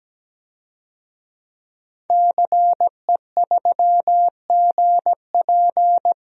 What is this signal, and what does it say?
Morse code, heard as a keyed tone, about 17 wpm, 700 Hz. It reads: CE3GP
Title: DXCC-100 Call Signs - 5 Characters - 17wpm